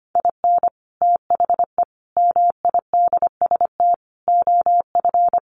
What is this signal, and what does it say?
Morse code, heard as a keyed tone, about 25 wpm, 700 Hz. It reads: IN THE MIDST OF
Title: Sets of 5 Words 25wpm